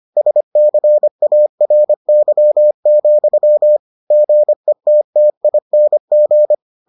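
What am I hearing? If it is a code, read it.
Morse code, heard as a keyed tone, about 25 wpm, 600 Hz. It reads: SCARY, GETTING
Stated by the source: Morse edition of the Foundations of Amateur Radio episodes.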